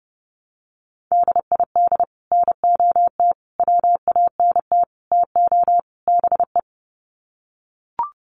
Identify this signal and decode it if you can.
Morse code, heard as a keyed tone, about 30 wpm, 700 Hz. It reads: DID NOT WANT TO BE
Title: Sets of 5 Words 30wpm